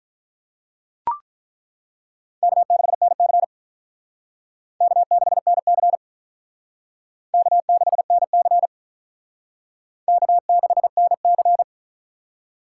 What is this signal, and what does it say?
Morse code, heard as a keyed tone, about 48 wpm, 700 Hz. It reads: K6NC K6NC K6NC K6NC